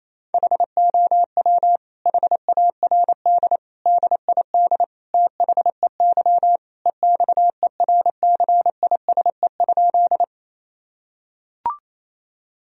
Morse code, heard as a keyed tone, about 28 wpm, 700 Hz. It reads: HOW HARD DID THEY EXERCISE?
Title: Sentences from Top 1000 Words - Review Part 4 - 28wpm